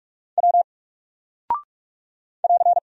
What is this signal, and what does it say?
Morse code, heard as a keyed tone, about 44 wpm, 700 Hz. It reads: W <AR>
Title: Single Character (Letter:Number:Punctuation:Prosign) 45wpm